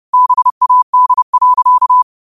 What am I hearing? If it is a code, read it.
Morse code, heard as a keyed tone, about 31 wpm, 1000 Hz. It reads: BAD.
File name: qotdfiles / 30wpm